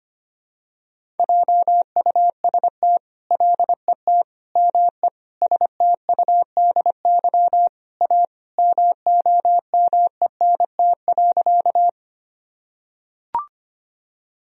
Morse code, heard as a keyed tone, about 25 wpm, 700 Hz. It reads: JUST LET ME STUDY A MOMENT.
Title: Sentences from Top 1000 Words - Review Part 1 - 25wpm